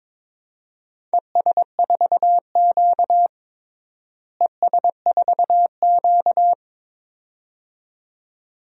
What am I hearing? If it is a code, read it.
Morse code, heard as a keyed tone, about 22 wpm, 700 Hz. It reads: ES4Q ES4Q
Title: DXCC-100 Call Signs - 4 Characters - Repeated 2x - 22wpm